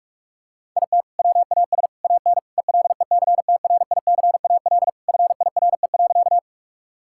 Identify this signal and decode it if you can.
Morse code, heard as a keyed tone, about 46 wpm, 700 Hz. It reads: IT WAS AN ELECTRICAL FIRE.
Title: Sentences from Top 1000 Words - Review Part 6 - 45wpm